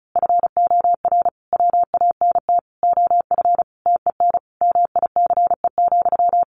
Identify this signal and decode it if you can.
Morse code, heard as a keyed tone, about 34 wpm, 700 Hz. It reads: FOR WANT OF TEN MICE,
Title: A Sound Of Thunder 35wpm